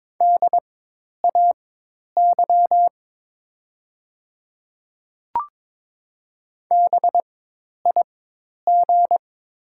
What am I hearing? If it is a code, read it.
Morse code, heard as a keyed tone, about 22 wpm, spaced as for 10 wpm, 690 Hz. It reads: DAY BIG
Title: 4x Character Spacing - Top 200 Words - 22wpm